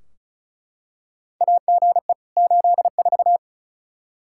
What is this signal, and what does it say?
Morse code, heard as a keyed tone, about 34 wpm, 700 Hz. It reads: AGE 84
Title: Practice QSO - Exchange 3 - More Info - 35wpm